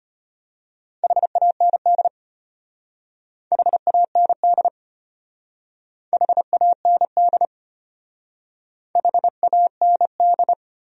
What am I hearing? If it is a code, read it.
Morse code, heard as a keyed tone, about 31 wpm, 700 Hz. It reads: HAND HAND HAND HAND
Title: Speed-Racing Top 300 Words 25wpm